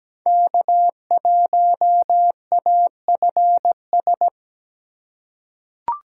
Morse code, heard as a keyed tone, about 17 wpm, 700 Hz. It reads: K1AFS
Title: USA Call Signs - 5 Characters - 3x Character Spacing - 17wpm